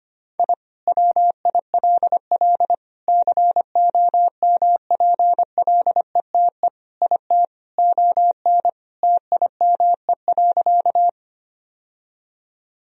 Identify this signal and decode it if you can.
Morse code, heard as a keyed tone, about 25 wpm, 700 Hz. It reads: I WILL COMPLETE IT ON TIME.
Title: Sentences from Top 1000 Words - Review Part 2 - 25wpm